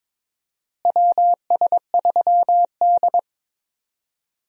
Morse code, heard as a keyed tone, about 23 wpm, 700 Hz. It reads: WS3D
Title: USA Call Signs - 4 Characters - Repeated 2x - 22wpm